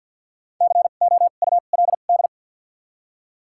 Morse code, heard as a keyed tone, about 53 wpm, 700 Hz. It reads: C O U L D
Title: 2x Character Spacing - Top 300 Words - 50wpm